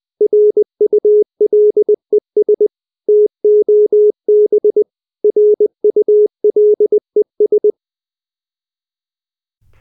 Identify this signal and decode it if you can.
Morse code, heard as a keyed tone, about 20 wpm, 420 Hz. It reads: RULES TOB RULES